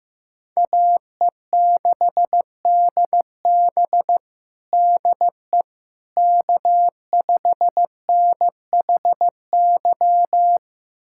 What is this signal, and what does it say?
Morse code, heard as a keyed tone, about 15 wpm, 700 Hz. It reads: AE6DB DE K5NHY
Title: USA Callsign DE Callsign - 5 Characters - 15wpm